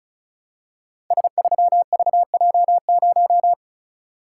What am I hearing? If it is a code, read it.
Morse code, heard as a keyed tone, about 36 wpm, 700 Hz. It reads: S3VJ0
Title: Code Groups - 5 Letter:Numbers - 35wpm